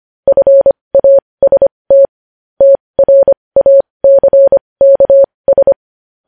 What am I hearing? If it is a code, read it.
Morse code, heard as a keyed tone, about 25 wpm, spaced as for 20 wpm, 550 Hz. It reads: FAST TRACKS
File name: podcast / News headlines-1x-25x20